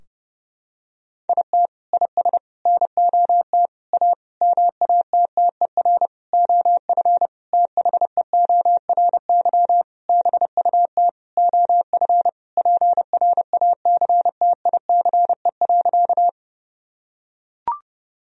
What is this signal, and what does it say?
Morse code, heard as a keyed tone, about 30 wpm, 700 Hz. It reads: IT IS NOT A MATTER OF THEORY BUT OF PRACTICE.
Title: Sentences from Top 1000 Words - Review Part 4 - 30wpm